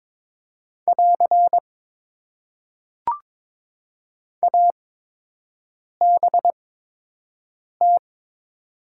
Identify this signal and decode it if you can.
Morse code, heard as a keyed tone, about 22 wpm, spaced as for 6 wpm, 700 Hz. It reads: <AR> ABT
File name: Top 90 QSO Elements - 8x Character Spacing - 22wpm